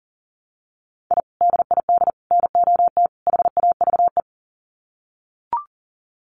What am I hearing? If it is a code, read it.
Morse code, heard as a keyed tone, about 40 wpm, 700 Hz. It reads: I DID NOT HAVE
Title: Sets of 4 Words 40wpm